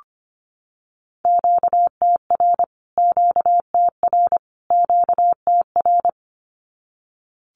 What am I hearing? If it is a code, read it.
Morse code, heard as a keyed tone, about 25 wpm, 700 Hz. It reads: QTR QTR QTR
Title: Q-Codes Repeated 3x 25wpm